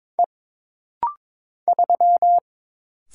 Morse code, heard as a keyed tone, about 22 wpm, 700 Hz. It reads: E 3
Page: This rapid-fire timing is geared to push your ICR (Instant Character Recognition) skills.